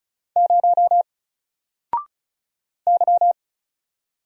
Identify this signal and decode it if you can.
Morse code, heard as a keyed tone, about 35 wpm, 700 Hz. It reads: 0 Y
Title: Single Letter-Number 35wpm